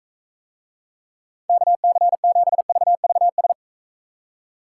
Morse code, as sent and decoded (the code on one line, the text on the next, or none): -.- -.-. --... ...- ...- ...
KC7VVS